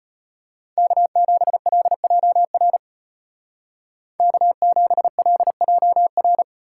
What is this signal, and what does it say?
Morse code, heard as a keyed tone, about 34 wpm, 700 Hz. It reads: K7LJR K7LJR